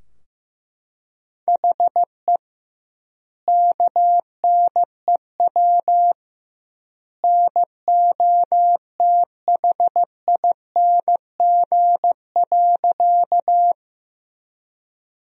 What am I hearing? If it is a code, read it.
Morse code, heard as a keyed tone, about 15 wpm, 700 Hz. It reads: HE KNEW NOTHING.
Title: 2x Word-Spacing - Sentences from Top 500 Words - 15wpm